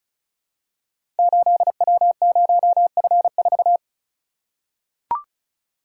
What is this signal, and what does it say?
Morse code, heard as a keyed tone, about 36 wpm, 700 Hz. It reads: 8W0F4
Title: Code Groups - 5 Letter:Numbers - 35wpm